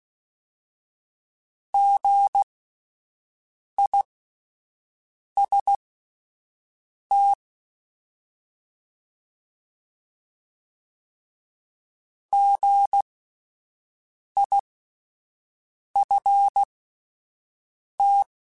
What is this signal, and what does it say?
Morse code, heard as a keyed tone, about 16 wpm, spaced as for 5 wpm, 800 Hz. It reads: GIST GIFT
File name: Spaced Alphas / Spaced Alphas mp3s / Gs sp @10